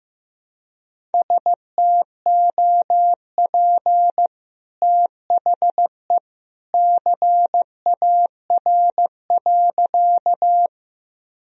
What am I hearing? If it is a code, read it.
Morse code, heard as a keyed tone, about 14 wpm, 700 Hz. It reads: STOP THE CAR.
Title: Sentences from Top 1000 Words - Review Part 1 - 15wpm